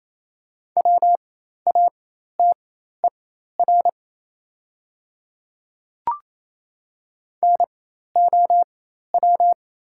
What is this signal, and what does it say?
Morse code, heard as a keyed tone, about 28 wpm, spaced as for 13 wpm, 700 Hz. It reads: WATER NOW
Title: 4x Character Spacing - Top 100 Words - 28wpm